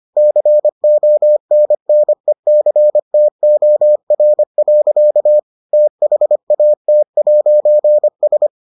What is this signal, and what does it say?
Morse code, heard as a keyed tone, about 25 wpm, 600 Hz. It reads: CONNECTOR. THAT'S